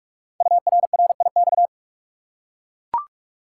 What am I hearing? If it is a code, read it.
Morse code, heard as a keyed tone, about 44 wpm, 700 Hz. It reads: URRIX
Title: Code Groups - 5 Letters - 45wpm